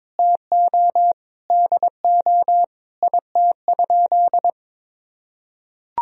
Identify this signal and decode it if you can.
Morse code, heard as a keyed tone, about 22 wpm, 700 Hz. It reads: TO DO IT?
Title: Sentences from Top 200 Words and Abbreviations 22wpm